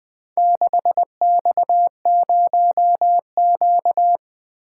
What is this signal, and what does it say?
Morse code, heard as a keyed tone, about 20 wpm, 700 Hz. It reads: 6X0Q
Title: Four Letter-Number Combinations - 20wpm